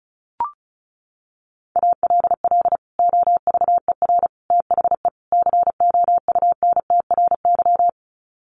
Morse code, heard as a keyed tone, about 34 wpm, 700 Hz. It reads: ALL OVER THE COUNTRY
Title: Sets of 4 Words 35wpm